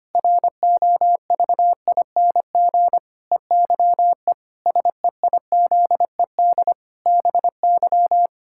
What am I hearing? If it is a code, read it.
Morse code, heard as a keyed tone, about 25 wpm, 700 Hz. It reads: ROVING EYE SEIZED BY
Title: Pangram Challenge v2 - 25wpm